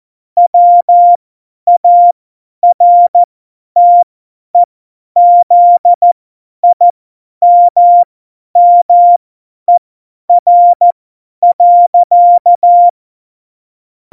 Morse code, heard as a keyed tone, about 14 wpm, spaced as for 10 wpm, 700 Hz. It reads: WARTEZIMMER.
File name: Morsekurs / Witze / Witz 27. Februar 2018, 50 F70